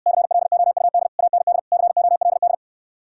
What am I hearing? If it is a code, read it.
Morse code, heard as a keyed tone, about 66 wpm, 700 Hz. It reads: CLOUD AND COLD